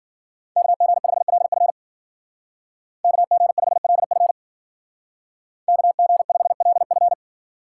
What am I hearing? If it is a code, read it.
Morse code, heard as a keyed tone, about 51 wpm, 700 Hz. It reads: KG5LF KG5LF KG5LF